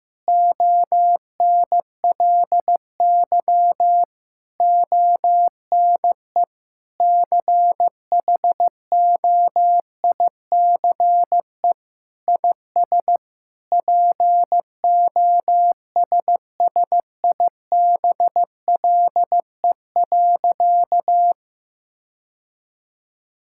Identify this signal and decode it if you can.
Morse code, heard as a keyed tone, about 15 wpm, 700 Hz. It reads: ONLY ONE CHOICE IS POSSIBLE.
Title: Sentences from Top 1000 Words - Review Part 6 - 15wpm